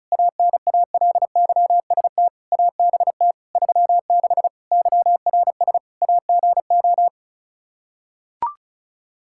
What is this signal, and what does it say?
Morse code, heard as a keyed tone, about 36 wpm, 700 Hz. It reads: ANALYST ABT 36 YRS AGO
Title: Practice QSO - Exchange 3 - More Info - 35wpm